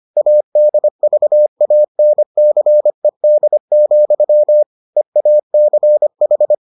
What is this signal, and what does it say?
Morse code, heard as a keyed tone, about 25 wpm, 600 Hz. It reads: ADVANCED, EACH